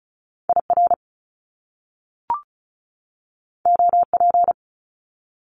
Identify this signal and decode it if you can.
Morse code, heard as a keyed tone, about 36 wpm, 700 Hz. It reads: IR OP
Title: Top Two Letter Combinations in Words 35wpm